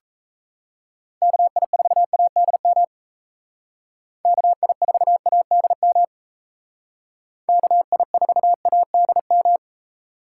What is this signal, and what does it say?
Morse code, heard as a keyed tone, about 36 wpm, 700 Hz. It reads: KI4ADM KI4ADM KI4ADM